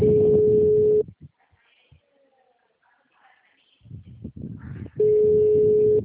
Ringback tone (Europe, 425 Hz).
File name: out- -2012- - - .91632